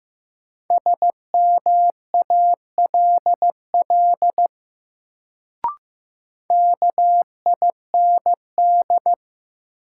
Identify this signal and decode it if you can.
Morse code, heard as a keyed tone, about 15 wpm, 700 Hz. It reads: SMALL KIND
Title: Sentences from Top 200 Words - New Word Drill - 15wpm